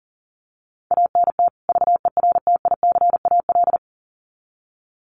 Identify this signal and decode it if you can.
Morse code, heard as a keyed tone, about 40 wpm, 700 Hz. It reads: ANT VERTICAL
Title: Common QSO Phrases 40wpm